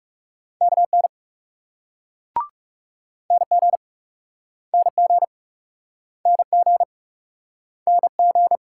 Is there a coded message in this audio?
Yes: Morse code from 0.6 to 8.6 s, heard as a keyed tone, about 40 wpm, 700 Hz. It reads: KN NG NG NG NG